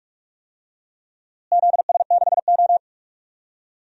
Morse code, heard as a keyed tone, about 46 wpm, 700 Hz. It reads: ZS6O
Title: DXCC-100 Call Signs - 4 Characters - Repeated 3x - 45wpm